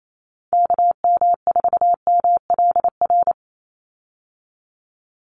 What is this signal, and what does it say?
Morse code, heard as a keyed tone, about 28 wpm, 700 Hz. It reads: KM4MLR
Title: USA Call Signs 28wpm